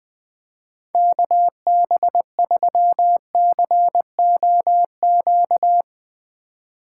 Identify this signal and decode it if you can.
Morse code, heard as a keyed tone, about 20 wpm, 700 Hz. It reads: KB3COQ
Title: USA Call Signs - 4x Character Spacing - 20wpm